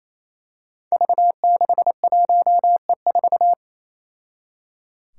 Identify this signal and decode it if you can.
Morse code, heard as a keyed tone, about 28 wpm, 700 Hz. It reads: V61E4
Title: Five Challenging Letter-Number Combinations 28wpm